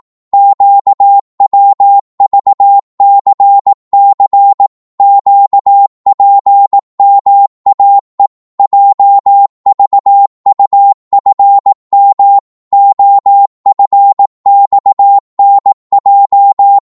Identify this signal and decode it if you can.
Morse code, heard as a keyed tone, about 18 wpm, 800 Hz. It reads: QWVCC QPMAE JVUFM OFXNJ
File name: Morse / ltr-18wpm